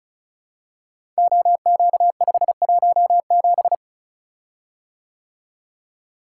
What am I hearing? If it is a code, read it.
Morse code, heard as a keyed tone, about 36 wpm, 700 Hz. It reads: OQ517
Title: Code Groups - 5 Letter:Numbers - 35wpm